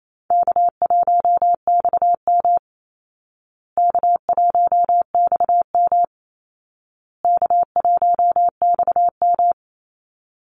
Morse code, heard as a keyed tone, about 28 wpm, 700 Hz. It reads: K1XM K1XM K1XM
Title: Easy Call Signs Repeated 3x - 28wpm